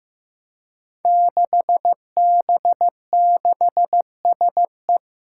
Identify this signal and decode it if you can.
Morse code, heard as a keyed tone, about 15 wpm, 700 Hz. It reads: 6B6SE
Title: Five Challenging Letter-Number Combinations 15wpm